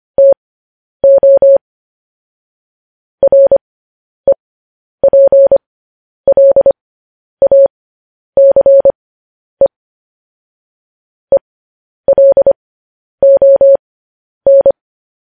Morse code, heard as a keyed tone, about 25 wpm, spaced as for 10 wpm, 550 Hz. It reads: TO REPLACE ELON
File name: podcast / News headlines-1x-25x10